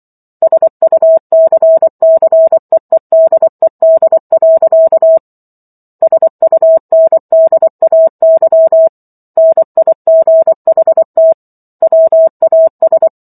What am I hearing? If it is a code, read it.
Morse code, heard as a keyed tone, about 24 wpm, 650 Hz. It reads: SUCCEEDED. SUNDAY NIGHT WAS